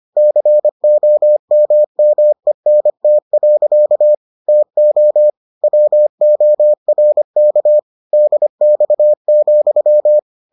Morse code, heard as a keyed tone, about 25 wpm, 600 Hz. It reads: COMMENT. TO WORK DX,